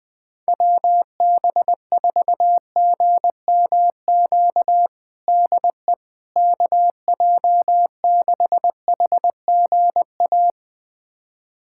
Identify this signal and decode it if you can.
Morse code, heard as a keyed tone, about 21 wpm, 700 Hz. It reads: WB4GMQ DE KJ6HGA
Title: USA Callsign DE Callsign - 6 Characters - 20wpm